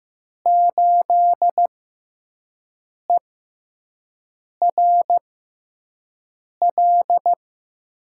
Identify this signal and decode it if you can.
Morse code, heard as a keyed tone, about 15 wpm, spaced as for 5 wpm, 700 Hz. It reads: 8ERL
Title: USA Call Signs - 5 Characters - 6x Character Spacing - 15wpm